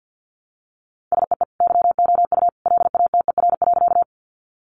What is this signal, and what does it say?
Morse code, heard as a keyed tone, about 49 wpm, 700 Hz. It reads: SEE YOU LATER.
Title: Sentences from Top 700 Words 50wpm